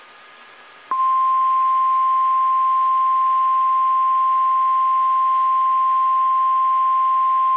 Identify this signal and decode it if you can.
Emergency Alert System (SAME) tones (the 1050 Hz attention tone). It is heard from 0.9 to 6.8 s.